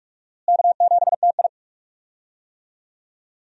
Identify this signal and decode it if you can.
Morse code, heard as a keyed tone, about 43 wpm, 700 Hz. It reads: K7TI
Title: USA Call Signs - 4 Characters - Repeated 3x - 45wpm